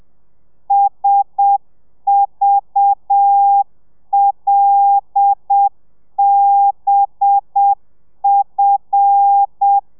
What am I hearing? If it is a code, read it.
Morse code, heard as a keyed tone, about 7 wpm, 800 Hz. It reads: SVLBF
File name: telegrafia / i2tfj / Lez5 30